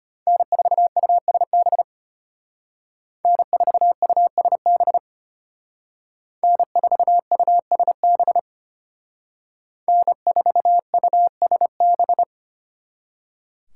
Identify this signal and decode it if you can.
Morse code, heard as a keyed tone, about 32 wpm, 700 Hz. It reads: N4USB N4USB N4USB N4USB